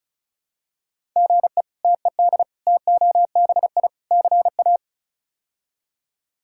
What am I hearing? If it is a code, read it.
Morse code, heard as a keyed tone, about 36 wpm, 700 Hz. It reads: GE TED TOBI CA
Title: K1USN SST QSO Practice - 35wpm